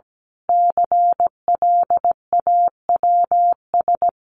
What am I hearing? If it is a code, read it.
Morse code, heard as a keyed tone, about 17 wpm, 700 Hz. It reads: CLAWS
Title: A Sound Of Thunder 17wpm